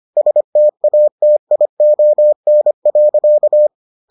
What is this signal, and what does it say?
Morse code, heard as a keyed tone, about 25 wpm, 600 Hz. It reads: STATION.